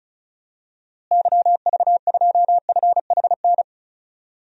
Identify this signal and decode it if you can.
Morse code, heard as a keyed tone, about 36 wpm, 700 Hz. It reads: YV2FHN
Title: DXCC-100 Call Signs - 6 Characters - Repeated 3x - 35wpm